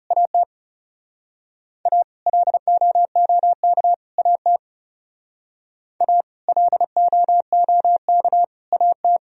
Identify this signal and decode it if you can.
Morse code, heard as a keyed tone, about 36 wpm, 700 Hz. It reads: AT A LOOK AT A LOOK AT
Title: Speed-Racing - Sets of 3 Words 30wpm